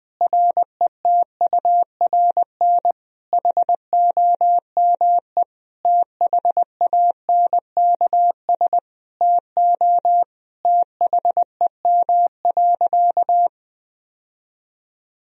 Morse code, heard as a keyed tone, about 20 wpm, 700 Hz. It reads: RETURN HOME THANKS TO THEM.
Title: Sentences from Top 1000 Words - Review Part 3 - 20wpm